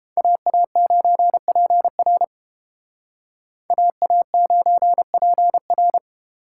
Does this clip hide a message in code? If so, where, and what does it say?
Yes: Morse code from 0.2 to 6.0 s, heard as a keyed tone, about 29 wpm, 700 Hz. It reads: AA9PR AA9PR